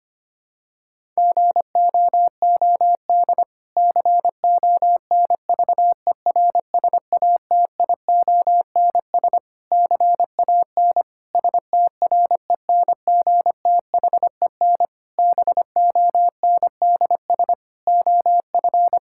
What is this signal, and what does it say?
Morse code, heard as a keyed tone, about 25 wpm, 700 Hz. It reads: GOOD CONVERSATIONS CAN STRENGTHEN BONDS OF
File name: practice / 25wpm / Difficult Sentences with Hints - 25wpm